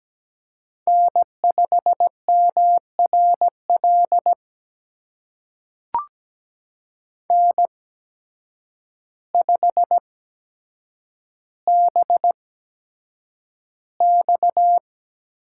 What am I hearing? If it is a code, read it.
Morse code, heard as a keyed tone, about 17 wpm, spaced as for 8 wpm, 700 Hz. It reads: N5MRL N5BX
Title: USA Call Signs - 8x Character Spacing - 17wpm